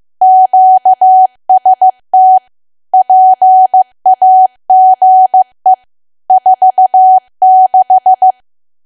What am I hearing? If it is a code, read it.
Morse code, heard as a keyed tone, about 15 wpm, 750 Hz. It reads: QST PAGE 46
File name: cwa / W1AW-Level2 / 15WPM